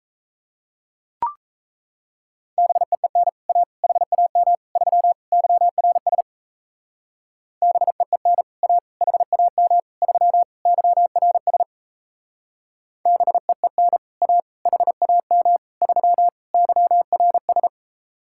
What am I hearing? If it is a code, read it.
Morse code, heard as a keyed tone, about 32 wpm, 700 Hz. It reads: BEEN A HAM 3 YRS BEEN A HAM 3 YRS BEEN A HAM 3 YRS
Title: Speed-Racing Common QSO Phrases - 28wpm